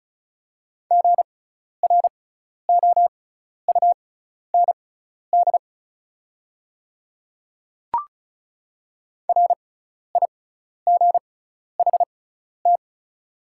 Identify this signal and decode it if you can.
Morse code, heard as a keyed tone, about 34 wpm, spaced as for 12 wpm, 690 Hz. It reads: GROUND RIGHT
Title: 6x Character Spacing - Top 500 Words - 35wpm